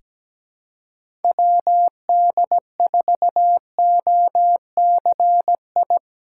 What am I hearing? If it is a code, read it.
Morse code, heard as a keyed tone, about 17 wpm, 700 Hz. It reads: WD4OCI
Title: USA Callsign DE Callsign - 6 Characters - 17wpm